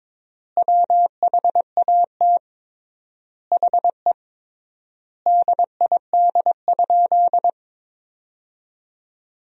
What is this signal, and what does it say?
Morse code, heard as a keyed tone, about 21 wpm, 700 Hz. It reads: WHAT HE DID?
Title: 3x Word-Spacing - Sentences from Top 100 Words - 22wpm